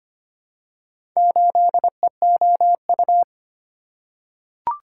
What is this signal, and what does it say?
Morse code, heard as a keyed tone, about 25 wpm, 700 Hz. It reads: 8EOU
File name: practice / 25wpm / Four Letter-Number Combinations - 25wpm